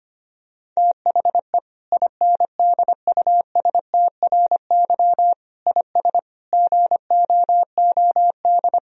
Morse code, heard as a keyed tone, about 25 wpm, 700 Hz. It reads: THE INDUSTRY IS GOOD
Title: Sentences from Top 1000 Words - Review Part 4 - 25wpm